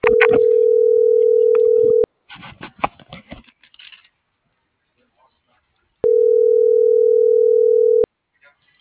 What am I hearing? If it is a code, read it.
Ringback tone (North America), 440 + 480 Hz.